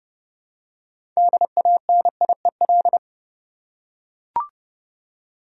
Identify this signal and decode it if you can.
Morse code, heard as a keyed tone, about 30 wpm, 700 Hz. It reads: DANIEL
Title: USA Common Names - 30wpm